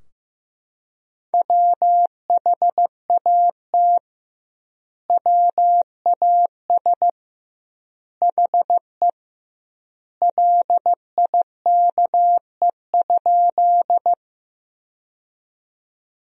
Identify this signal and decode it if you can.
Morse code, heard as a keyed tone, about 16 wpm, 700 Hz. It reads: WHAT WAS HE LIKE?
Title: 2x Word-Spacing - Sentences from Top 100 Words - 15wpm